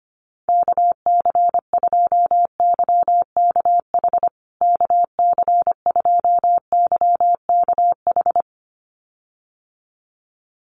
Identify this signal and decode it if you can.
Morse code, heard as a keyed tone, about 25 wpm, 700 Hz. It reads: KC2YKH KC2YKH
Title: USA Call Signs Repeated 2x 25wpm